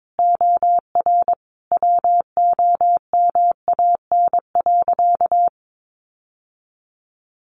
Morse code, heard as a keyed tone, about 22 wpm, 700 Hz. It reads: OR WOMAN.
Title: Sentences from Top 1400 Words 22wpm